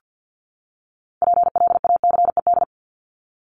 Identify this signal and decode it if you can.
Morse code, heard as a keyed tone, about 50 wpm, 700 Hz. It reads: PLACED
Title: ED Suffix - Top 100 Words with Base Words 50wpm